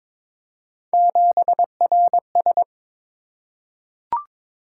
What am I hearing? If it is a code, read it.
Morse code, heard as a keyed tone, about 22 wpm, 700 Hz. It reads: 7RS
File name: practice / 22wpm / Three Letter-Number Combinations - 22wpm